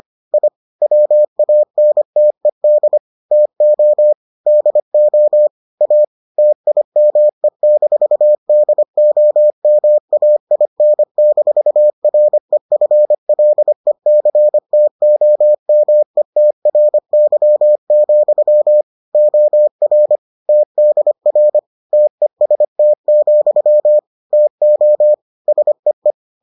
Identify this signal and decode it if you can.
Morse code, heard as a keyed tone, about 25 wpm, 600 Hz. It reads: I WANTED TO DO A TIME-DOMAIN-REFLECTOMETRY, OR TDR TEST, TO SEE